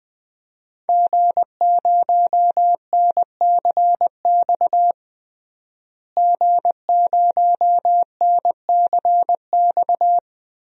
Morse code, heard as a keyed tone, about 21 wpm, 700 Hz. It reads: G0NCX G0NCX
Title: DXCC-100 Call Signs - 5 Characters - Repeated 3x - 20wpm